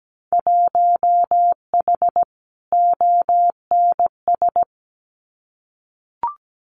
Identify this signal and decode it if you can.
Morse code, heard as a keyed tone, about 17 wpm, 700 Hz. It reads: 1H ONS
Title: Winter Field Day Exchanges 17wpm